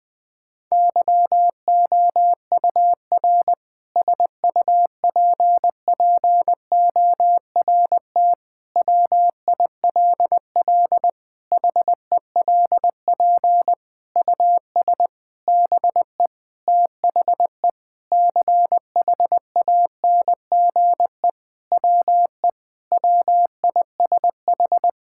Morse code, heard as a keyed tone, about 20 wpm, 700 Hz. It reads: YOUR SUPPORT WILL HELP US BE THE CHANGE WE WISH
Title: Sentences with 7-Word Phrases - Spoken Beginning - 20wpm